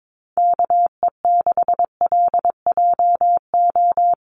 Morse code, heard as a keyed tone, about 22 wpm, 700 Hz. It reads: KE6LJO
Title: USA Call Signs 22wpm